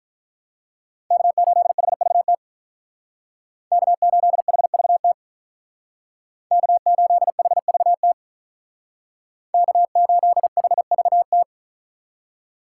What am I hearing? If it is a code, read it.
Morse code, heard as a keyed tone, about 46 wpm, 700 Hz. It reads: K8HVT K8HVT K8HVT K8HVT